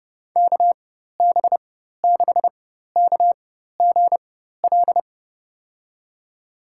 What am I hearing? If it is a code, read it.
Morse code, heard as a keyed tone, about 32 wpm, spaced as for 14 wpm, 700 Hz. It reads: KB6KGL
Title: USA Call Signs - 6 Characters - 4x Character Spacing - 30wpm